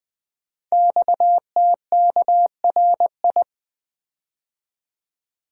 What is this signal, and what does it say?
Morse code, heard as a keyed tone, about 20 wpm, 700 Hz. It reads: XTKRI
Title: Code Groups - 5 Letters - 20wpm